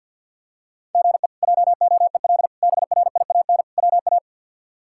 Morse code, heard as a keyed tone, about 53 wpm, 700 Hz. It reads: GE JOEL BRIAN WA
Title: K1USN SST QSO Practice - 50wpm